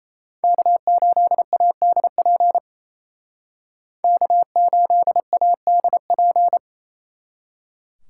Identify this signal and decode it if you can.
Morse code, heard as a keyed tone, about 36 wpm, 700 Hz. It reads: K8ADP K8ADP